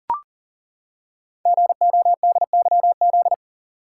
Morse code, heard as a keyed tone, about 38 wpm, 700 Hz. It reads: GODYZ
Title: Code Groups - 5 Letters - 40wpm